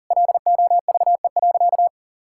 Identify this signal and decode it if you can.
Morse code, heard as a keyed tone, about 40 wpm, 700 Hz. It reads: LOVE.
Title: Sentences from Top 1600 Words 40wpm